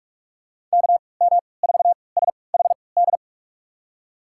Morse code, heard as a keyed tone, about 46 wpm, spaced as for 25 wpm, 700 Hz. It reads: KM4SHD
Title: USA Call Signs - 3x Character Spacing - 45wpm